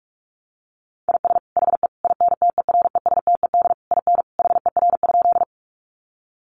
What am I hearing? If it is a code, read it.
Morse code, heard as a keyed tone, about 44 wpm, 700 Hz. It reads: IS HE INTERESTED IN HER?
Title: Sentences from Top 900 Words 45wpm